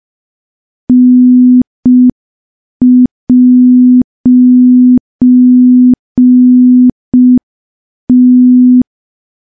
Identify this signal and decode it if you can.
Morse code, heard as a keyed tone, about 5 wpm, 250 Hz. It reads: N'T
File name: morse / morse-training-05WPM